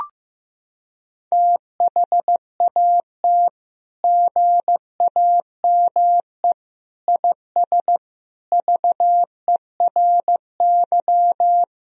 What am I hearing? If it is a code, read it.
Morse code, heard as a keyed tone, about 15 wpm, 700 Hz. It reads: THAT GAME IS VERY
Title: Sentences from Top 1000 Words - Review Part 3 - 15wpm